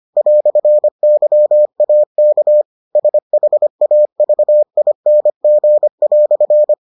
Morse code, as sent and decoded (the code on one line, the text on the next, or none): .-..-. -.-- .- -.- / ... .... .- ...- .. -. --. .-..-.
"YAK SHAVING"